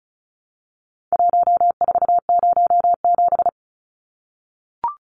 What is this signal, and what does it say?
Morse code, heard as a keyed tone, about 34 wpm, 700 Hz. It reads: 1407
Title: Numbers - Contest Style - 4-Digit 35wpm